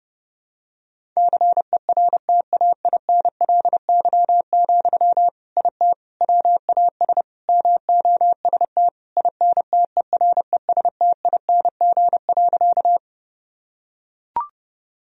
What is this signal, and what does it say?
Morse code, heard as a keyed tone, about 30 wpm, 700 Hz. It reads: CERTAINLY, IT WAS MOST INTERESTING.
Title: Sentences from Top 1000 Words - Review Part 1 - 30wpm